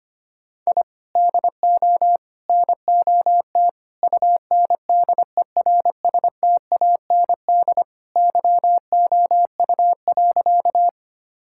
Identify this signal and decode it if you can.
Morse code, heard as a keyed tone, about 25 wpm, 700 Hz. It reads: I DO NOT UNDERSTAND YOU.
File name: practice / 25wpm / Sentences with 4-Word Phrases - 25wpm